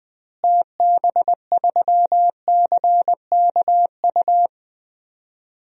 Morse code, heard as a keyed tone, about 20 wpm, 700 Hz. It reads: TB3CKU
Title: DXCC-100 Call Signs - 6 Characters - Repeated 2x - 20wpm